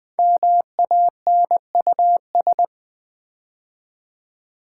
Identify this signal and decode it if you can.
Morse code, heard as a keyed tone, about 20 wpm, 700 Hz. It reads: MANUS
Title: Build A Word - 5 Letters At A Time - 10-Letter Words - 20wpm